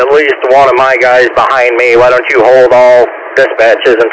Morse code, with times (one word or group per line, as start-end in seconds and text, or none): none